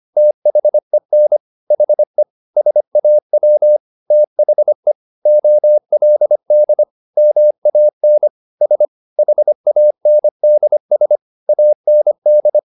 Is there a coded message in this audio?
Yes: Morse code from 0.2 to 12.6 s, heard as a keyed tone, about 25 wpm, 600 Hz. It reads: THEN HE SAW THE OLD MAN S HANDS AND